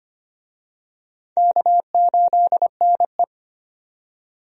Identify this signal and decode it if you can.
Morse code, heard as a keyed tone, about 25 wpm, 700 Hz. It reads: K8NE
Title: USA Call Signs - 4 Characters - 25wpm